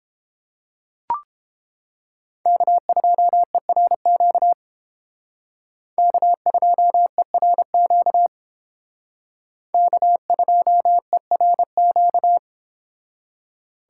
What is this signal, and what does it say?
Morse code, heard as a keyed tone, about 29 wpm, 700 Hz. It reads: K2ERQ K2ERQ K2ERQ